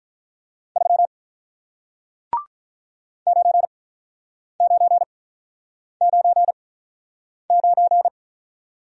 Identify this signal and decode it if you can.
Morse code, heard as a keyed tone, about 48 wpm, spaced as for 9 wpm, 700 Hz. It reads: 3 9999